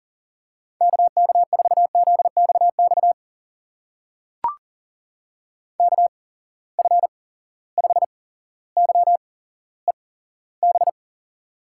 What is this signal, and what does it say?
Morse code, heard as a keyed tone, about 40 wpm, spaced as for 17 wpm, 700 Hz. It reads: KK4ZXX KF5YEB
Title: USA Call Signs - 6 Characters - 8x Character Spacing - 40wpm